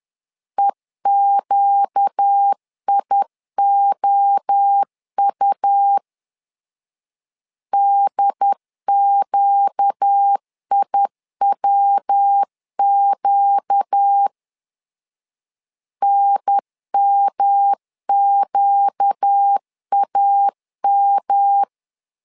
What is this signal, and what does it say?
Morse code, heard as a keyed tone, about 10 wpm, 800 Hz. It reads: EQIOU DQIWQ NMQAM
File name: LEZIONI / 4 Q 10wpm 2210